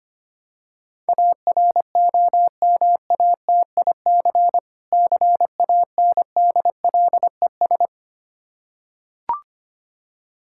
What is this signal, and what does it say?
Morse code, heard as a keyed tone, about 25 wpm, 700 Hz. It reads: AROMATIC CANDLES
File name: practice / 25wpm / Difficult Words with Hints - 25wpm